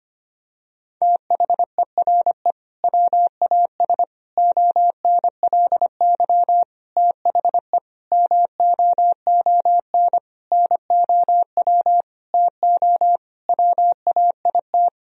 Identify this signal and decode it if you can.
Morse code, heard as a keyed tone, about 25 wpm, 700 Hz. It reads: THERE WAS ONLY THE MOON NOW TO WAIT
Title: Sentences from Top 1000 Words - Review Part 3 - 25wpm